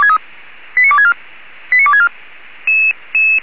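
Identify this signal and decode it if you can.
Five-tone selective call (ZVEI), heard as quick runs of five tones, tones of about 70 ms. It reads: 89264 89264
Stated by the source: ZVEI five-tone selective call received on an SDR.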